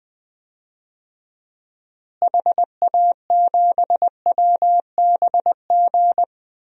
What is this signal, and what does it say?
Morse code, heard as a keyed tone, about 20 wpm, 700 Hz. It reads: HA7WBG
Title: DXCC-100 Call Signs - 6 Characters - Repeated 2x - 20wpm